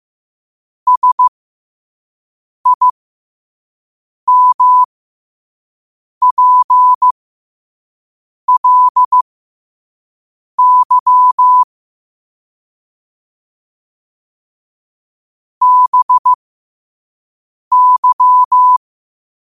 Morse code, heard as a keyed tone, about 15 wpm, spaced as for 5 wpm, 1000 Hz. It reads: SIMPLY BY
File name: qotdfiles / 5wpm